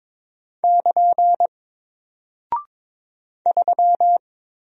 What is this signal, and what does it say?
Morse code, heard as a keyed tone, about 22 wpm, 700 Hz. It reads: <KN> 3
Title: Single Character (Letter:Number:Punctuation:Prosign) 22wpm